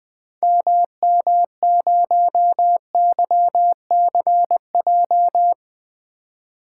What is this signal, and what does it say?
Morse code, heard as a keyed tone, about 20 wpm, 700 Hz. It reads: MM0YCJ
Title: 2023 SOTA Activator Call Signs - 20wpm